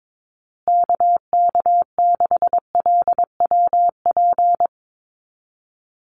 Morse code, heard as a keyed tone, about 22 wpm, 700 Hz. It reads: KK6LWP
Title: USA Call Signs Repeated 2x 22wpm